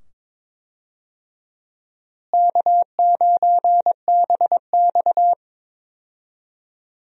Morse code, heard as a keyed tone, about 22 wpm, 700 Hz. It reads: K9BX
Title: USA Call Signs - 4 Characters - Repeated 2x - 22wpm